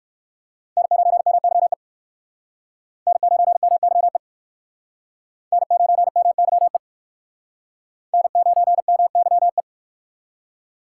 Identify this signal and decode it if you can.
Morse code, heard as a keyed tone, about 58 wpm, 700 Hz. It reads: N9MYE N9MYE N9MYE N9MYE